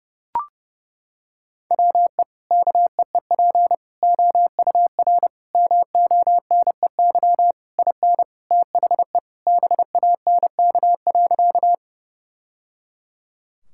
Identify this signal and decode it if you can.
Morse code, heard as a keyed tone, about 30 wpm, 700 Hz. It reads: WE KEEP OUR MONEY IN THE BANK.
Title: Sentences from Top 1000 Words - Review Part 6 - 30wpm